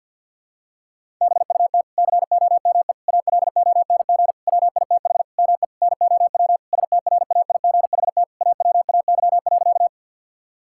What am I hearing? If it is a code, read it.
Morse code, heard as a keyed tone, about 50 wpm, 700 Hz. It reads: BUT COME ALONG WITH ME NOW STRAIGHT AWAY.
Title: Sentences from Top 1000 Words - Review Part 6 - 50wpm